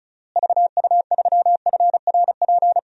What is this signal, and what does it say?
Morse code, heard as a keyed tone, about 36 wpm, 700 Hz. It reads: VU3FRP
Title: DXCC-100 Call Signs - 6 Characters - Repeated 3x - 35wpm